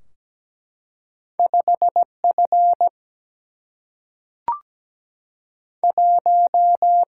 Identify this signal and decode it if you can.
Morse code, heard as a keyed tone, about 17 wpm, spaced as for 3 wpm, 700 Hz. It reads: 5 F1
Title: Single-Letter Single-Number 17wpm